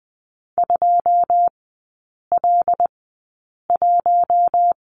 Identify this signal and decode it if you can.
Morse code, heard as a keyed tone, about 20 wpm, spaced as for 8 wpm, 700 Hz. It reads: 2L1
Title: 100 Lesson 17e - Letter u - Three Characters 20wpm